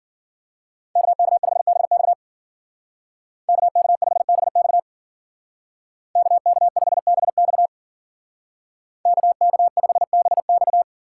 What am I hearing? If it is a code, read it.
Morse code, heard as a keyed tone, about 49 wpm, 700 Hz. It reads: KK5BX KK5BX KK5BX KK5BX
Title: Speed Racing - USA Call Signs 40wpm